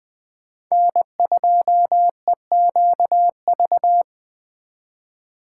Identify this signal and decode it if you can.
Morse code, heard as a keyed tone, about 20 wpm, 700 Hz. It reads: N2EQV